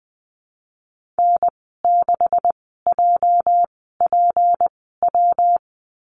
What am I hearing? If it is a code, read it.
Morse code, heard as a keyed tone, about 19 wpm, spaced as for 14 wpm, 700 Hz. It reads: N6JPW
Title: 238 Lesson 38g - Letter q - Call Signs 20wpm